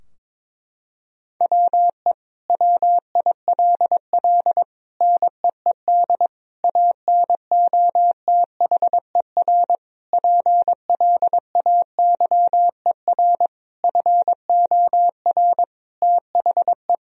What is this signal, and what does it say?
Morse code, heard as a keyed tone, about 22 wpm, 700 Hz. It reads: WE WILL NEED ANOTHER PLAYER FOR THE
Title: Sentences from Top 1000 Words - Review Part 4 - 22wpm